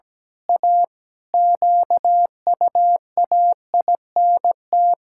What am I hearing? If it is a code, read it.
Morse code, heard as a keyed tone, about 17 wpm, 700 Hz. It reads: A QUAINT
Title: Pangram Challenge v2 - 17wpm